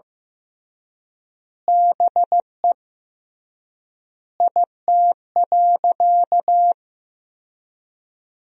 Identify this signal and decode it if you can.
Morse code, heard as a keyed tone, about 15 wpm, 700 Hz. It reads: BE IT.
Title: 3x Word-Spacing - Sentences from Top 100 Words - 15wpm